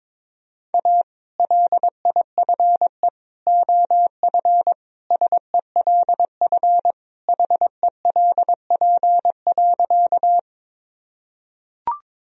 Morse code, heard as a keyed tone, about 22 wpm, 700 Hz. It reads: A LIFE OF SELF HELP.
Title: Sentences from Top 1000 Words - Review Part 1 - 22wpm